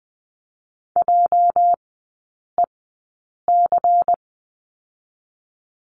Morse code, heard as a keyed tone, about 20 wpm, spaced as for 8 wpm, 700 Hz. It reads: JEC
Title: 218 Lesson 35e - Number 0 - Three Characters 20wpm